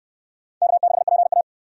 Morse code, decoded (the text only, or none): K6PA